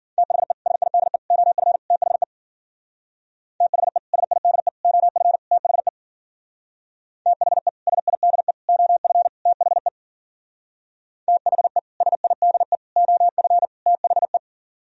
Morse code, heard as a keyed tone, about 52 wpm, 700 Hz. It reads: THE SIDE OF THE THE SIDE OF THE THE SIDE OF THE THE SIDE OF THE